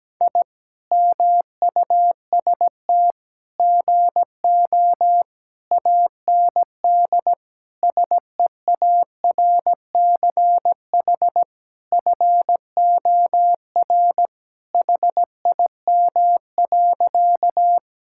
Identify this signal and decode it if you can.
Morse code, heard as a keyed tone, about 17 wpm, 700 Hz. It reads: I MUST GO AND SEARCH FOR HIM.
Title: Sentences from Top 1000 Words - Review Part 5 - 17wpm